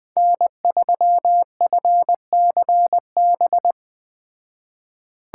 Morse code, heard as a keyed tone, about 20 wpm, 700 Hz. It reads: N3FCB
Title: Speed Racing - USA 1x3 Call Signs 17wpm